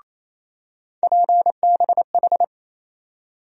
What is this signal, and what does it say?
Morse code, heard as a keyed tone, about 28 wpm, 700 Hz. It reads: PBH
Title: USA Call Sign Postfixes - 3 Letters 28wpm